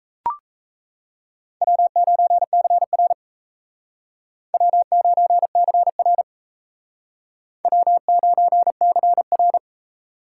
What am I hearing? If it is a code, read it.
Morse code, heard as a keyed tone, about 34 wpm, 700 Hz. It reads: W9CR W9CR W9CR
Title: Speed Racing - USA 1x2 Call Signs 28wpm